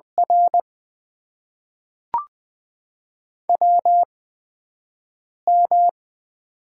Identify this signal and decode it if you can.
Morse code, heard as a keyed tone, about 20 wpm, spaced as for 5 wpm, 700 Hz. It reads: R WM
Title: USA Call Signs - 4 Characters - 8x Character Spacing - 20wpm